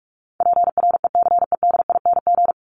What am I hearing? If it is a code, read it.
Morse code, heard as a keyed tone, about 44 wpm, 700 Hz. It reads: PRECEDING
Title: ING Suffix - Top 100 Words with Base Words 45wpm